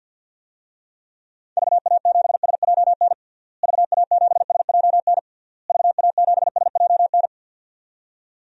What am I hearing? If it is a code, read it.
Morse code, heard as a keyed tone, about 52 wpm, 700 Hz. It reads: VA7SJN VA7SJN VA7SJN
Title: DXCC-100 Call Signs - 6 Characters - Repeated 3x - 50wpm